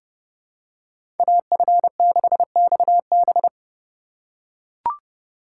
Morse code, heard as a keyed tone, about 30 wpm, 700 Hz. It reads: AF6XB
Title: USA Call Signs - 5 Characters - 2x Character Spacing - 30wpm